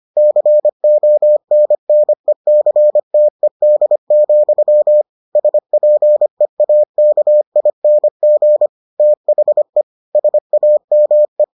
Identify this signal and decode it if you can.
Morse code, heard as a keyed tone, about 25 wpm, 600 Hz. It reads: CONNECTED, SPEAKING THE SAME